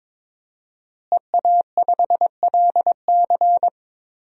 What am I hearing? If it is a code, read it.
Morse code, heard as a keyed tone, about 22 wpm, 700 Hz. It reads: EA5LC
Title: DXCC-100 Call Signs - 5 Characters - Repeated 2x - 22wpm